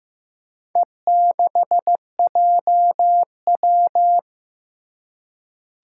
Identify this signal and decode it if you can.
Morse code, heard as a keyed tone, about 15 wpm, 700 Hz. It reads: E6JW
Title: Four Challenging Letter-Number Combinations 15wpm